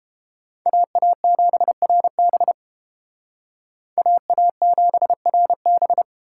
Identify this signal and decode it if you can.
Morse code, heard as a keyed tone, about 30 wpm, 700 Hz. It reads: AA7RB AA7RB